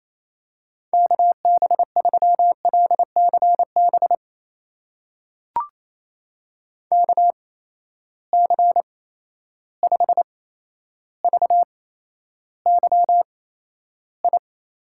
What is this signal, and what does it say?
Morse code, heard as a keyed tone, about 28 wpm, spaced as for 12 wpm, 700 Hz. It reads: KB3LCB KC5VYI
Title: USA Call Signs - 6 Characters - 8x Character Spacing - 28wpm